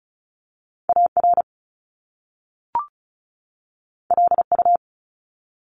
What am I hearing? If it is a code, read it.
Morse code, heard as a keyed tone, about 36 wpm, 700 Hz. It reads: AR LU
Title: Top Two Letter Combinations in Words 35wpm